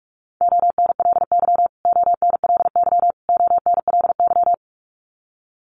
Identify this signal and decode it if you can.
Morse code, heard as a keyed tone, about 44 wpm, 700 Hz. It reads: ONLY ONLY ONLY
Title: Top 500 Words Repeated 45wpm